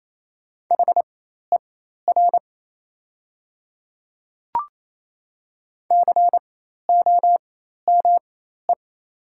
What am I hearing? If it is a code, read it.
Morse code, heard as a keyed tone, about 27 wpm, spaced as for 13 wpm, 690 Hz. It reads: HER COME
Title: 4x Character Spacing - Top 100 Words - 28wpm